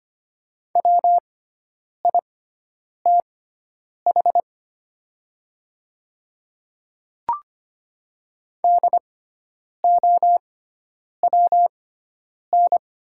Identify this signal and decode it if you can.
Morse code, heard as a keyed tone, about 25 wpm, spaced as for 9 wpm, 700 Hz. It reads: WITH DOWN
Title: 6x Character Spacing - Top 100 Words - 25wpm